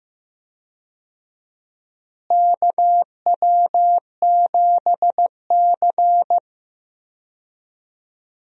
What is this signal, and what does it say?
Morse code, heard as a keyed tone, about 15 wpm, 700 Hz. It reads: KW7C
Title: USA Call Signs - 4 Characters - Repeated 2x - 15wpm